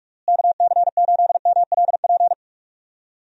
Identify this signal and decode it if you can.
Morse code, heard as a keyed tone, about 46 wpm, 700 Hz. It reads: KC8MLP
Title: USA Call Signs - 6 Characters - Repeated 3x - 45wpm